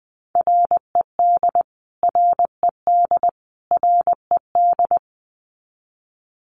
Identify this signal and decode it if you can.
Morse code, heard as a keyed tone, about 20 wpm, 700 Hz. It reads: RED RED RED
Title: Top 500 Words Repeated 20wpm